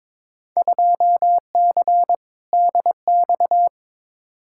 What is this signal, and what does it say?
Morse code, heard as a keyed tone, about 22 wpm, 700 Hz. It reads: 2C DX
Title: Field Day Exchanges Encore - 22wpm